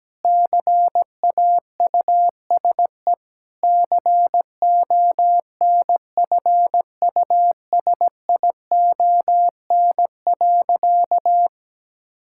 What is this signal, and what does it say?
Morse code, heard as a keyed tone, about 17 wpm, 700 Hz. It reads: CAUSE CONFUSION.
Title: Sentences from Top 1600 Words 17wpm